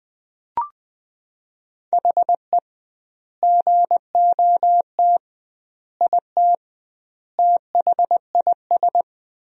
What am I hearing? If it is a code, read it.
Morse code, heard as a keyed tone, about 20 wpm, 700 Hz. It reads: HE GOT IT THIS
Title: 2x Word-Spacing - Sentences from Top 400 Words - 20wpm